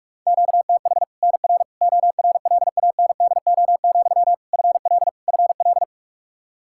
Morse code, heard as a keyed tone, about 46 wpm, 700 Hz. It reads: QTH NR ORLANDO, FL FL
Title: Practice QSO - Exchange 1 - Beginning - 45wpm